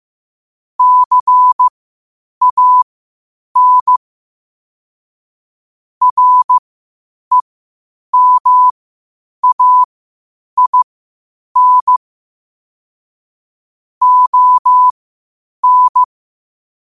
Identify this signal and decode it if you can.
Morse code, heard as a keyed tone, about 15 wpm, spaced as for 8 wpm, 1000 Hz. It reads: CAN REMAIN ON
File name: qotdfiles / 8wpm